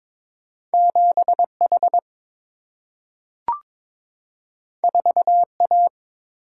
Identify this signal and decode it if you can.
Morse code, heard as a keyed tone, about 22 wpm, 700 Hz. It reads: 7H 4A
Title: Single-Letter Single-Number 22wpm